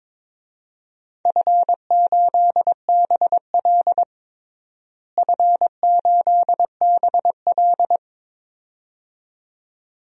Morse code, heard as a keyed tone, about 22 wpm, 700 Hz. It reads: F8BL F8BL
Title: DXCC-100 Call Signs - 4 Characters - Repeated 2x - 22wpm